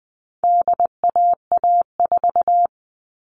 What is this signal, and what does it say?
Morse code, heard as a keyed tone, about 20 wpm, 700 Hz. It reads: DAA4
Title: All Cut Numbers - 4-Digit 20wpm